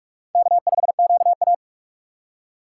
Morse code, heard as a keyed tone, about 44 wpm, 700 Hz. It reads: K5QA
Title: USA Call Signs - 6x Character Spacing - 45wpm